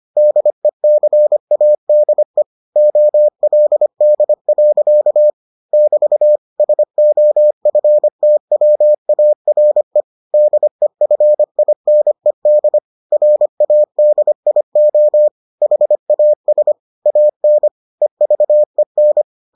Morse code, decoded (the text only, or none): DECADE OLD. <BT> SOFTWARE DEFINED RADIO HAS AN EVEN